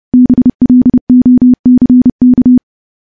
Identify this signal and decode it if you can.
Morse code, heard as a keyed tone, about 30 wpm, 250 Hz. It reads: BLOCK
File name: morse / morse-training-30WPM